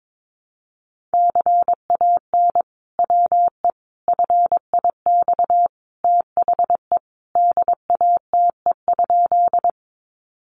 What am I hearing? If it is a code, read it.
Morse code, heard as a keyed tone, about 22 wpm, 700 Hz. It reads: CAN WE FIX THE DATE?
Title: Sentences from Top 1400 Words 22wpm